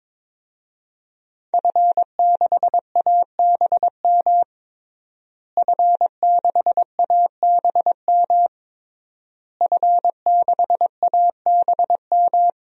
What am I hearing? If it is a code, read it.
Morse code, heard as a keyed tone, about 22 wpm, 700 Hz. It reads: F6ABM F6ABM F6ABM
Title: DXCC-100 Call Signs - 5 Characters - Repeated 3x - 22wpm